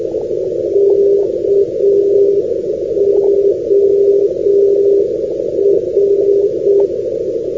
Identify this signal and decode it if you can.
Morse code, heard as a keyed tone, about 6 wpm, 400 Hz, with radio noise. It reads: KOR
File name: europe / grc / grc-252 ndb korinthos kor-392-np-grc-athinai-2012-06-rb